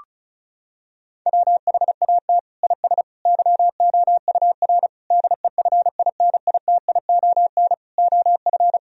Morse code, heard as a keyed tone, about 36 wpm, 700 Hz. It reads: WHAT IS YOUR DEFINITION OF
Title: Sentences from Top 1000 Words - Review Part 4 - 35wpm